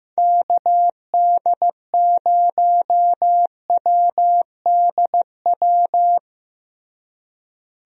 Morse code, heard as a keyed tone, about 15 wpm, 700 Hz. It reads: KD0WDW
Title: USA Call Signs - 6 Characters - Repeated 3x - 15wpm